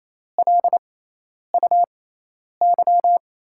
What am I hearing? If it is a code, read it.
Morse code, heard as a keyed tone, about 27 wpm, spaced as for 10 wpm, 700 Hz. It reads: LUY
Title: USA Call Signs - 6 Characters - 6x Character Spacing - 28wpm